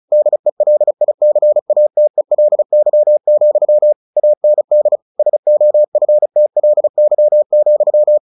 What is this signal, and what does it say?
Morse code, heard as a keyed tone, about 34 wpm, 600 Hz. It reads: DELICATELY, AND SOFTLY,
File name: the old man and the sea / the old man and the sea 1 35wpm